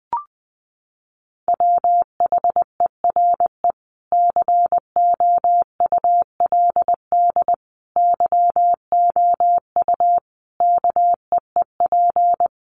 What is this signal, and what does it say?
Morse code, heard as a keyed tone, about 20 wpm, 700 Hz. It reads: WHERE COULD YOU KEEP
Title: Sentences from Top 300 Words 20wpm